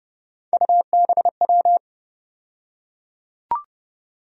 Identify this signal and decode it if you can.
Morse code, heard as a keyed tone, about 29 wpm, 700 Hz. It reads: UBW
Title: USA Call Sign Postfixes - 3 Letters 30wpm